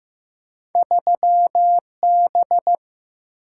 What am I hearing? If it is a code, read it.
Morse code, heard as a keyed tone, about 15 wpm, 700 Hz. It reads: 3B
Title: Single-Letter Single-Number 15wpm